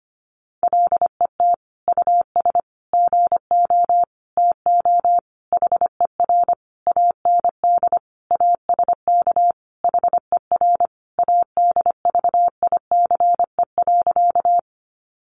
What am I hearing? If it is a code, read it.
Morse code, heard as a keyed tone, about 25 wpm, 700 Hz. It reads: LET US GO TO HER AND ASK HER ADVICE.
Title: Sentences from Top 1100 Words 25wpm